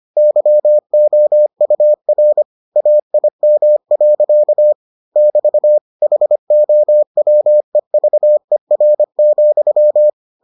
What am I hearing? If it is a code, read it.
Morse code, heard as a keyed tone, about 25 wpm, 600 Hz. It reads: YOUR AIM. <BT> HOWEVER,